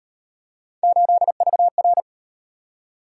Morse code, heard as a keyed tone, about 38 wpm, 700 Hz. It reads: 8VR